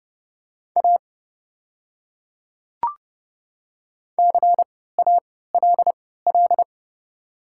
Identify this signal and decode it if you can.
Morse code, heard as a keyed tone, about 30 wpm, spaced as for 17 wpm, 700 Hz. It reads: A CALL
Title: 3x Character Spacing - Top 200 Words - 30wpm